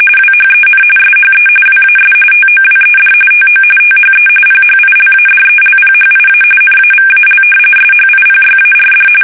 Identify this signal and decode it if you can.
RTTY, about 45.45 baud. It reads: WHEN SHE WAS TOWED TO THE PHILADELPHIA NAVAL SHIPYARD
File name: fsk850 / battleshipnj